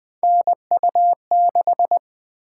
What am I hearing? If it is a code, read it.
Morse code, heard as a keyed tone, about 20 wpm, 700 Hz. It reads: NU6
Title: Three Letter-Number Combinations - 20wpm